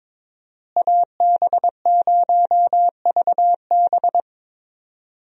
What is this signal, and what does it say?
Morse code, heard as a keyed tone, about 22 wpm, 700 Hz. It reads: AB0VB
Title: USA Call Signs - 5 Characters - Repeated 2x - 22wpm